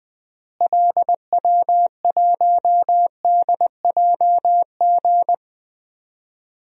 Morse code, heard as a keyed tone, about 20 wpm, 690 Hz. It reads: LW1DJG
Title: DXCC-100 Call Signs - 6 Characters - Repeated 2x - 20wpm